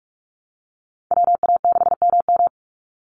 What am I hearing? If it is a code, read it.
Morse code, heard as a keyed tone, about 44 wpm, 700 Hz. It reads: WA6MM
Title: NA SOTA Activator Callsigns 45wpm